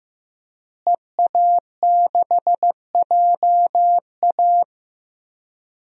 Morse code, heard as a keyed tone, about 15 wpm, 700 Hz. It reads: EA6JA
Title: DXCC-100 Call Signs - 5 Characters - Repeated 2x - 15wpm